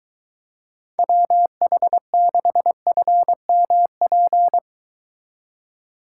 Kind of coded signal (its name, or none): Morse code